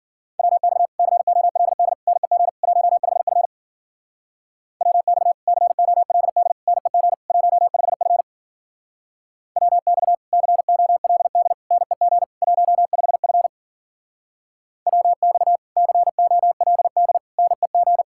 Morse code, heard as a keyed tone, about 49 wpm, 700 Hz. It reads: WX COLD NEG 15F WX COLD NEG 15F WX COLD NEG 15F WX COLD NEG
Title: Speed-Racing Common QSO Phrases - 40wpm